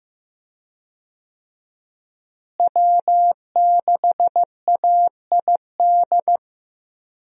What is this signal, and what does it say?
Morse code, heard as a keyed tone, about 15 wpm, 700 Hz. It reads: W6AID
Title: USA Call Signs - 5 Characters - Repeated 3x - 15wpm